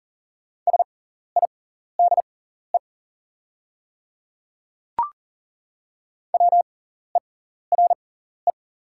Morse code, heard as a keyed tone, about 39 wpm, spaced as for 14 wpm, 690 Hz. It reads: SIDE WERE
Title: 6x Character Spacing - Top 100 Words - 40wpm